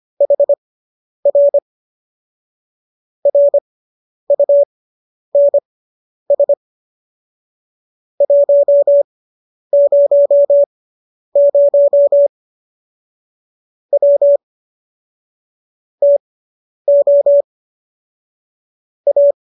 Morse code, heard as a keyed tone, about 25 wpm, spaced as for 10 wpm, 570 Hz. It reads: HR RUNS 100 W TO A